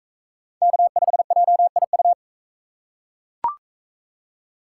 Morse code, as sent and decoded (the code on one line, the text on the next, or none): -.- ..... .--- .. ..-
K5JIU